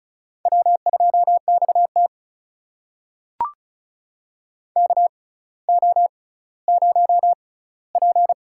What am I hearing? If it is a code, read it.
Morse code, heard as a keyed tone, about 36 wpm, spaced as for 18 wpm, 700 Hz. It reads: W2XT KO0P
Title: USA Call Signs - 4 Characters - 6x Character Spacing - 35wpm